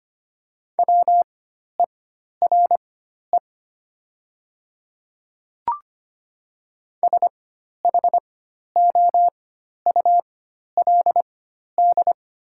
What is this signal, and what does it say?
Morse code, heard as a keyed tone, about 25 wpm, spaced as for 12 wpm, 700 Hz. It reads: WERE SHOULD
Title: 4x Character Spacing - Top 300 Words - 25wpm